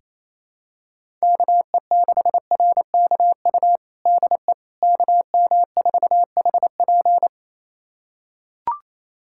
Morse code, heard as a keyed tone, about 28 wpm, 700 Hz. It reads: KE6RKU DE KM4HP
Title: USA Callsign DE Callsign 28wpm